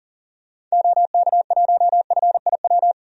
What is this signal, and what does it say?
Morse code, heard as a keyed tone, about 40 wpm, 700 Hz. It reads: OK1FIW
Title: DXCC-100 Call Signs - 6 Characters - 40wpm